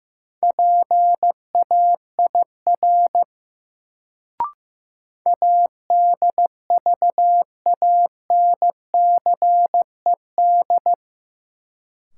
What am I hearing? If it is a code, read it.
Morse code, heard as a keyed tone, about 15 wpm, 700 Hz. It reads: PAIR ADVANCED
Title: Sentences from Top 1200 Words - New Word Drill - 15wpm